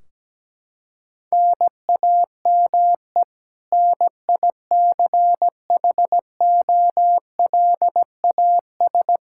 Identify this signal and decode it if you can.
Morse code, heard as a keyed tone, about 17 wpm, 700 Hz. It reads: NAME NICHOLAS
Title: K1USN SST QSO Practice 17wpm